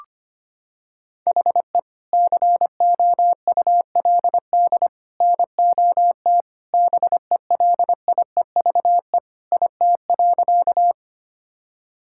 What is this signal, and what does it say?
Morse code, heard as a keyed tone, about 25 wpm, 700 Hz. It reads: HE COULD NOT BELIEVE IT.
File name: practice / 25wpm / Sentences with 3-Word Phrases - 25wpm